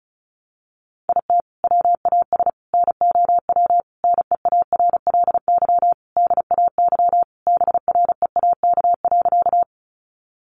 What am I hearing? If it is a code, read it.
Morse code, heard as a keyed tone, about 34 wpm, 700 Hz. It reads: IT WAS NOW NEARLY DAY BREAK.
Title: Sentences from Top 1300 Words 35wpm